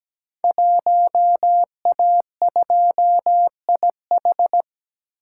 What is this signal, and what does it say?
Morse code, heard as a keyed tone, about 17 wpm, 700 Hz. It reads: 1A2IH
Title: Code Groups - 5 Letter:Numbers - 17wpm